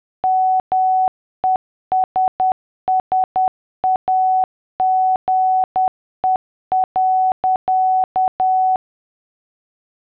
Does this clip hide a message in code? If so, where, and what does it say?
Yes: Morse code from 0.2 to 8.8 s, heard as a keyed tone, about 10 wpm, 750 Hz. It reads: MESSAGE.